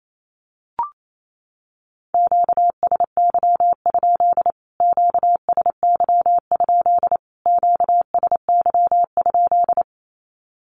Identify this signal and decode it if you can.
Morse code, heard as a keyed tone, about 28 wpm, 700 Hz. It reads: QSY? QSY? QSY?
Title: Q-Codes Repeated 3x 28wpm